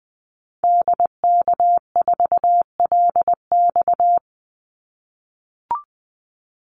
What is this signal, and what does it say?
Morse code, heard as a keyed tone, about 20 wpm, 700 Hz. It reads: DK4LX
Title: Call Signs - 20wpm